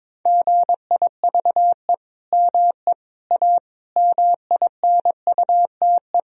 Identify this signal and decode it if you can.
Morse code, heard as a keyed tone, about 22 wpm, 700 Hz. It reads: GIVE ME A MINUTE
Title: Sentences from Top 1000 Words - Review Part 4 - 22wpm